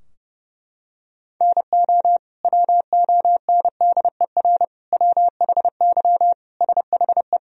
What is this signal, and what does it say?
Morse code, heard as a keyed tone, about 30 wpm, 700 Hz. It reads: NO WONDER WHY SHE
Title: Sentences from Top 1000 Words - Review Part 3 - 30wpm